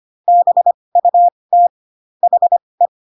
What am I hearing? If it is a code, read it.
Morse code, heard as a keyed tone, about 25 wpm, spaced as for 20 wpm, 700 Hz. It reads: BUT HE
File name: Pinocchio / Pinocchio ch6 15 25 ik1hge